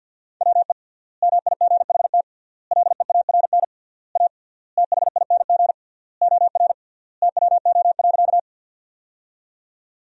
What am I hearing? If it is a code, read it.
Morse code, heard as a keyed tone, about 53 wpm, 700 Hz. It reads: WE MIGHT LEARN A THING OR TWO.
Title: 3x Word-Spacing - Sentences from Top 300 Words - 50wpm